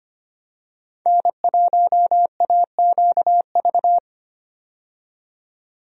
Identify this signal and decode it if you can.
Morse code, heard as a keyed tone, about 25 wpm, 700 Hz. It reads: N1AQV
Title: USA Call Signs - 5 Characters - 25wpm